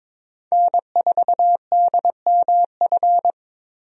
Morse code, heard as a keyed tone, about 22 wpm, 700 Hz. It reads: N4DMF
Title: USA Call Signs - 5 Characters - 22wpm